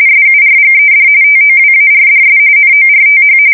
RTTY, about 45.45 baud. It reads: SURGED TOWARDS NORTH
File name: fsk170 / USS-New-Jersey